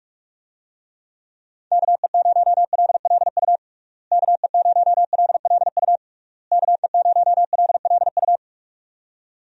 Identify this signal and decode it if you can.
Morse code, heard as a keyed tone, about 46 wpm, 700 Hz. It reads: KE0LLU KE0LLU KE0LLU
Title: USA Call Signs - 6 Characters - Repeated 3x - 45wpm